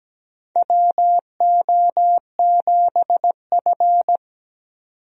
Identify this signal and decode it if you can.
Morse code, heard as a keyed tone, about 17 wpm, 700 Hz. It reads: WO7F
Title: USA Call Signs - 4 Characters - 17wpm